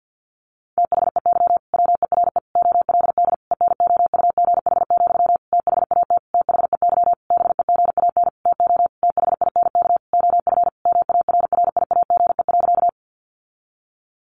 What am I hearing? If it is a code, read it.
Morse code, heard as a keyed tone, about 50 wpm, 700 Hz. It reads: THEY WERE OLD ENOUGH, THAT THEY BEGAN TO THINK OF MARRIAGE.
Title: Sentences from Top 800 Words 50wpm